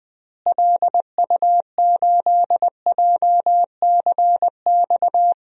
Morse code, heard as a keyed tone, about 20 wpm, 700 Hz. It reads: LU8JCX
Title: DXCC-100 Call Signs - 6 Characters - Repeated 3x - 20wpm